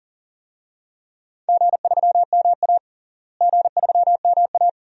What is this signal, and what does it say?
Morse code, heard as a keyed tone, about 40 wpm, 700 Hz. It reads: G3MA G3MA
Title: DXCC-100 Call Signs - 4 Characters - Repeated 2x - 40wpm